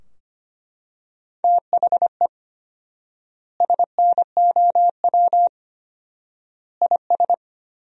Morse code, heard as a keyed tone, about 25 wpm, 700 Hz. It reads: THE SNOW IS
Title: 4x Word-Spacing - Sentences from Top 500 Words - 25wpm